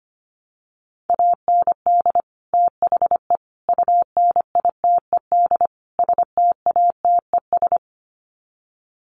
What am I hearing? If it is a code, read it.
Morse code, heard as a keyed tone, about 25 wpm, 700 Hz. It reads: AND THE UNITED STATES
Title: Sets of 4 Words 25wpm